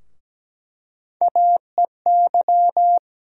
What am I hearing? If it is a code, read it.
Morse code, heard as a keyed tone, about 17 wpm, 700 Hz. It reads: AEY
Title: USA Call Sign Postfixes - 3 Letters 17wpm